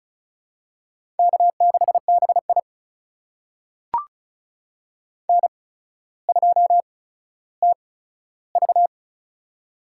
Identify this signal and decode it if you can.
Morse code, heard as a keyed tone, about 36 wpm, spaced as for 15 wpm, 700 Hz. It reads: K6BI N2TV
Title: USA Call Signs - 4 Characters - 8x Character Spacing - 35wpm